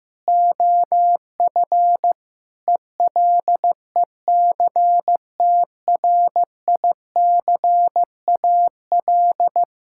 Morse code, heard as a keyed tone, about 15 wpm, 700 Hz. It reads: OF ELECTRICAL
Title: Sentences from Top 1000 Words - Review Part 6 - 15wpm